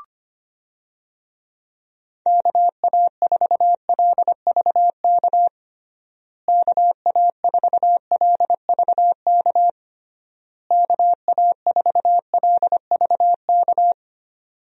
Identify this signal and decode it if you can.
Morse code, heard as a keyed tone, about 25 wpm, 700 Hz. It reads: KA4LVK KA4LVK KA4LVK
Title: USA Call Signs - 6 Characters - Repeated 3x - 25wpm